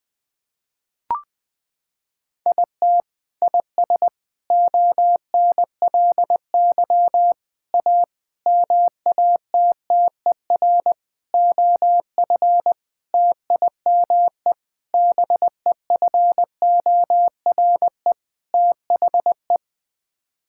Morse code, heard as a keyed tone, about 20 wpm, 700 Hz. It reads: IT IS ONLY A MATTER OF TIME BEFORE THE
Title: Sets of 9 Words - 20wpm